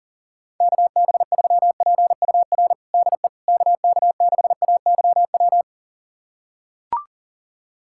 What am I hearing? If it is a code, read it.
Morse code, heard as a keyed tone, about 40 wpm, 700 Hz. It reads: KB3PUR DE KK6AYW
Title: USA Callsign DE Callsign - 6 Characters - 40wpm